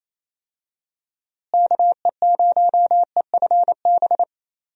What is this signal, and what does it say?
Morse code, heard as a keyed tone, about 28 wpm, 700 Hz. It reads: KE0EFB
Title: USA Call Signs - 6 Characters - Repeated 3x - 28wpm